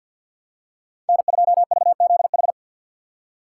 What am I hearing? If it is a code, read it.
Morse code, heard as a keyed tone, about 53 wpm, 700 Hz. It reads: N2VZH
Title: USA Call Signs - 6x Character Spacing - 50wpm